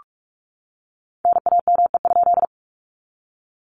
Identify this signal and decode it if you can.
Morse code, heard as a keyed tone, about 44 wpm, 700 Hz. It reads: NAME?
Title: Common QSO Phrases Encore 45wpm